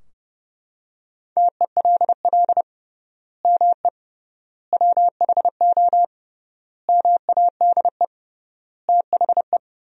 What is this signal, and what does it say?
Morse code, heard as a keyed tone, about 32 wpm, 700 Hz. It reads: TELL ME WHO MADE THE
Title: 3x Word-Spacing - Sentences from Top 200 Words - 30wpm